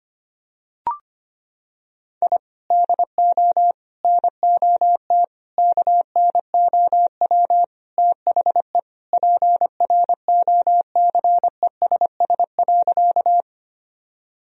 Morse code, heard as a keyed tone, about 25 wpm, 700 Hz. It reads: I DO NOT KNOW THE PROCESS.
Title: Sentences from Top 1000 Words - Review Part 3 - 25wpm